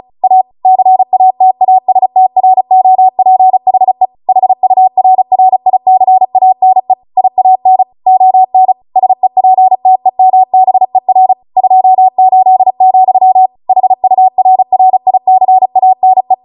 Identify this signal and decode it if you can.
Morse code, heard as a keyed tone, about 36 wpm, 750 Hz. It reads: A CATASTROPHE HURRICANE IAN ON SEPTEMBER 28, HURRICANE